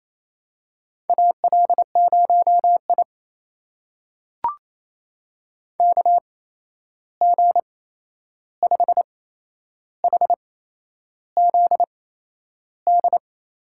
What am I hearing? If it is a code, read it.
Morse code, heard as a keyed tone, about 28 wpm, spaced as for 11 wpm, 700 Hz. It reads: AL0I KG5HZD
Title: USA Call Signs - 8x Character Spacing - 28wpm